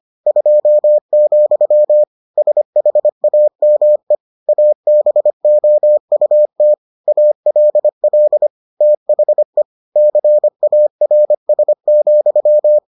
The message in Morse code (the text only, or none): 2, SHAME ABOUT ALL THE CARS,